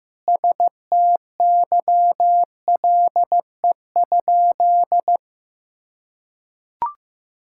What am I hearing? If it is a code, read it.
Morse code, heard as a keyed tone, about 15 wpm, 700 Hz. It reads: STYLE?
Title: Sentences from Top 1000 Words - Review Part 5 - 15wpm